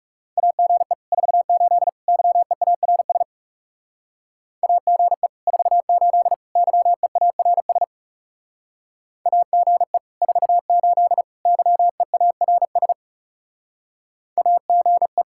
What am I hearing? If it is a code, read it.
Morse code, heard as a keyed tone, about 35 wpm, 700 Hz. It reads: AGE 48 YEARS AGE 48 YEARS AGE 48 YEARS AGE
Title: Speed-Racing Common QSO Phrases - 30wpm